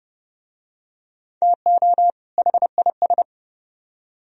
Morse code, heard as a keyed tone, about 24 wpm, 700 Hz. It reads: TO HIS
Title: Speed-Racing - Sets of 2 Words 20wpm